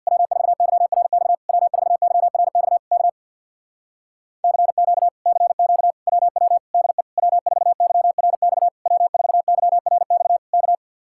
Morse code, heard as a keyed tone, about 60 wpm, 700 Hz. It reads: W4YRX W4YRX K CQ CQ WW DE W4YRX W4YRX K
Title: Speed-Racing Common QSO Phrases - 50wpm